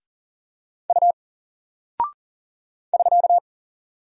Morse code, heard as a keyed tone, about 40 wpm, 700 Hz. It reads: U <SK>
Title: Single Character (Letter:Number:Punctuation:Prosign) Alt 40wpm